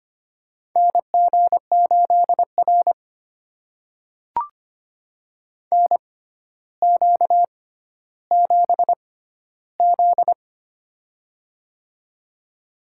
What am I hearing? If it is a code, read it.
Morse code, heard as a keyed tone, about 25 wpm, spaced as for 13 wpm, 700 Hz. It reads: NG8R NQ7Z
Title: USA Call Signs - 4 Characters - 6x Character Spacing - 25wpm